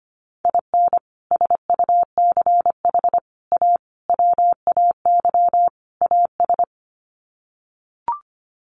Morse code, heard as a keyed tone, about 25 wpm, 700 Hz. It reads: IN SUCH A WAY AS
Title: Sets of 5 Words Encore 25wpm